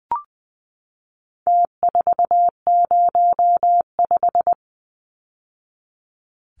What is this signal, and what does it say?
Morse code, heard as a keyed tone, about 20 wpm, 700 Hz. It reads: T405
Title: Numbers - Contest Style - 4-Digit 20wpm